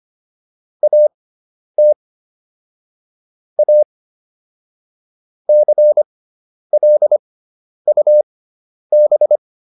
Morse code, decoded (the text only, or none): AT A CLUB